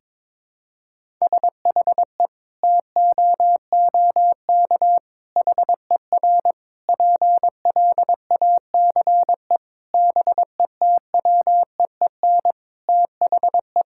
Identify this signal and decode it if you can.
Morse code, heard as a keyed tone, about 22 wpm, 700 Hz. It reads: SHE TOOK HER PLACE BETWEEN THE
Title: Sentences from Top 1000 Words - Review Part 5 - 22wpm